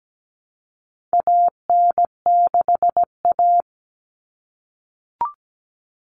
Morse code, heard as a keyed tone, about 17 wpm, 700 Hz. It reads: AN6A
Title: Common Cut Numbers - 4-Digit 17wpm